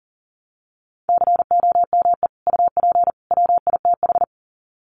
Morse code, heard as a keyed tone, about 40 wpm, 700 Hz. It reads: COME UP WITH
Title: Sets of 3 Words 40wpm